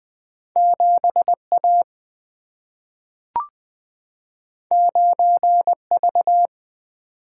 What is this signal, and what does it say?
Morse code, heard as a keyed tone, about 20 wpm, 700 Hz. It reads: 7A 9V
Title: Two Letter-Number Combinations - 20wpm